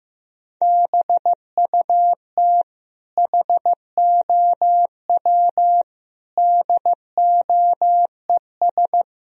Morse code, heard as a keyed tone, about 15 wpm, 700 Hz. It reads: BUT HOW DOES
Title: Sentences from Top 1000 Words - Review Part 3 - 15wpm